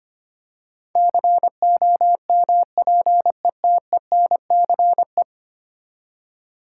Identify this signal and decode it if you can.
Morse code, heard as a keyed tone, about 25 wpm, 700 Hz. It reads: COMPETENCE
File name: Ten Letter Words - 25wpm